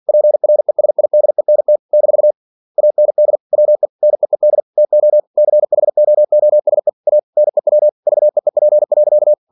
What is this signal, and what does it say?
Morse code, heard as a keyed tone, about 46 wpm, 600 Hz. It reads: PRESIDENT - AND WE NEED TO CHOOSE A NEW VEEP.
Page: Never 47wpm